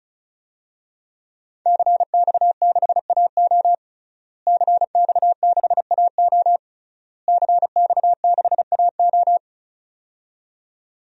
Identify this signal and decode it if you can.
Morse code, heard as a keyed tone, about 36 wpm, 700 Hz. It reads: CX6AO CX6AO CX6AO
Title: DXCC-100 Call Signs - 5 Characters - Repeated 3x - 35wpm